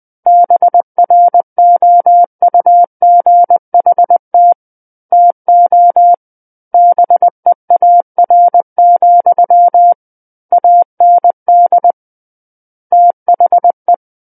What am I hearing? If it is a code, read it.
Morse code, heard as a keyed tone, about 20 wpm, 700 Hz. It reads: BROUGHT TO BEAR, AND THE
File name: 20 WPM